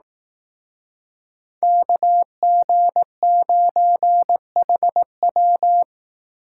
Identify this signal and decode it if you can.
Morse code, heard as a keyed tone, about 19 wpm, 700 Hz. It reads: KG9HW
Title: Speed Racing - USA 2x2 Call Signs 15wpm